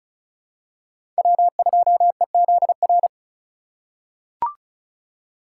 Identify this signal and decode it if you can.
Morse code, heard as a keyed tone, about 34 wpm, 700 Hz. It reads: W2EZR
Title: USA Call Signs - 5 Characters - 6x Character Spacing - 35wpm